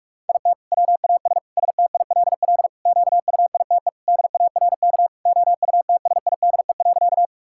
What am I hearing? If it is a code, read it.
Morse code, heard as a keyed tone, about 46 wpm, 700 Hz. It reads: IT WAS STILL QUITE DARK OUTSIDE.
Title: Sentences from Top 1000 Words - Review Part 6 - 45wpm